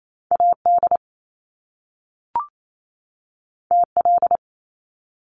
Morse code, heard as a keyed tone, about 28 wpm, 700 Hz. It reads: AD TL
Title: Top Two Letter Combinations in Words 28wpm